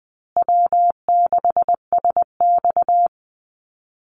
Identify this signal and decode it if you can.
Morse code, heard as a keyed tone, about 20 wpm, 700 Hz. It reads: W6SX
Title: Easy Call Signs Repeated 2x - 20wpm